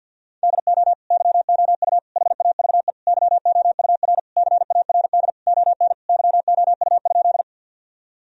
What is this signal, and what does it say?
Morse code, heard as a keyed tone, about 50 wpm, 700 Hz. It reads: DO YOU HAVE YOUR CARD ON YOU?
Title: Sentences from Top 1000 Words - Review Part 4 - 50wpm